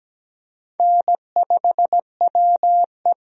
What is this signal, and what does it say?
Morse code, heard as a keyed tone, about 17 wpm, 700 Hz. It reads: N5WE
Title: USA Call Signs - 4 Characters - 17wpm